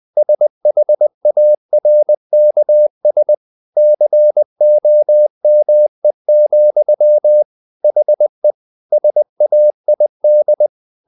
Morse code, heard as a keyed tone, about 20 wpm, 600 Hz. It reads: SHARKS COME, HE SAID
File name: the old man and the sea / the old man and the sea part 1 20wpm